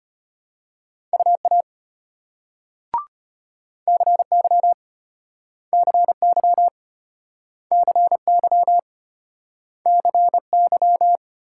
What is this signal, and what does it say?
Morse code, heard as a keyed tone, about 32 wpm, 700 Hz. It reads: UA CY CY CY CY